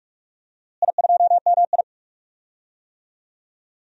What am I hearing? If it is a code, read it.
Morse code, heard as a keyed tone, about 46 wpm, 700 Hz. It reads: I2MI
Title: DXCC-100 Call Signs - 4 Characters - Repeated 2x - 45wpm